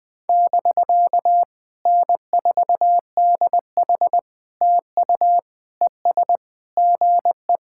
Morse code, heard as a keyed tone, about 20 wpm, 700 Hz. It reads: <BK> N4DH TU ES GE
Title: POTA QSO Practice - 20wpm